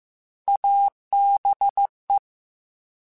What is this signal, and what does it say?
Morse code, heard as a keyed tone, about 15 wpm, 800 Hz. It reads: ABE
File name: Alphabetical / Alphabetical mp3s / As @13